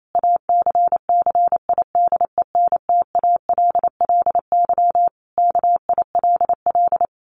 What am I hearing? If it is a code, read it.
Morse code, heard as a keyed tone, about 28 wpm, 700 Hz. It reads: ACCIDENTALLY KILL
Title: A Sound Of Thunder 28wpm